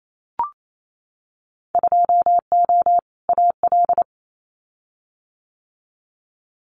Morse code, heard as a keyed tone, about 28 wpm, 700 Hz. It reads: 2O AL
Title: Winter Field Day Exchanges 28wpm